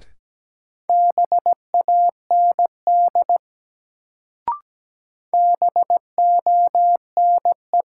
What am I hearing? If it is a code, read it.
Morse code, heard as a keyed tone, about 17 wpm, 700 Hz. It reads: BAND BONE
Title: Sentences from Top 1600 Words - New Word Drill Alt - 17wpm